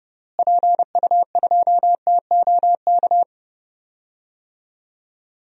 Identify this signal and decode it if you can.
Morse code, heard as a keyed tone, about 31 wpm, 700 Hz. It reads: PU2TOK
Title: DXCC-100 Call Signs - 6 Characters - Repeated 3x - 30wpm